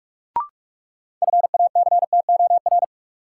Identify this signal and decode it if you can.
Morse code, heard as a keyed tone, about 46 wpm, 700 Hz. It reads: FACTOR
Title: Sentences from Top 1300 Words - New Word Drill - 45wpm